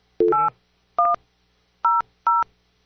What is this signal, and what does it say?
DTMF, heard as telephone touch tones, tones of about 180 ms. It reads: 8100